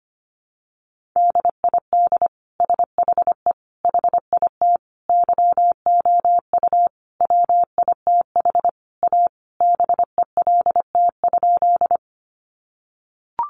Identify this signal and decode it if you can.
Morse code, heard as a keyed tone, about 25 wpm, 700 Hz. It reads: DID SHE HIT YOU WITH A BELT?
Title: Sentences from Top 1400 Words 25wpm